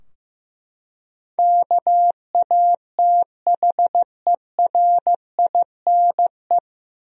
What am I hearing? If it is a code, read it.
Morse code, heard as a keyed tone, about 15 wpm, 700 Hz. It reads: KATHERINE
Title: USA Common Names - 15wpm